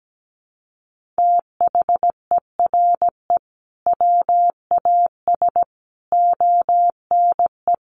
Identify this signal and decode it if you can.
Morse code, heard as a keyed tone, about 17 wpm, 700 Hz. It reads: THERE WAS ONE
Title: Sentences from Top 1500 Words 17wpm